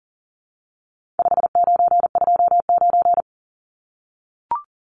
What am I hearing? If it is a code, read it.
Morse code, heard as a keyed tone, about 40 wpm, 700 Hz. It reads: 5929
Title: Numbers - 4-Digit 40wpm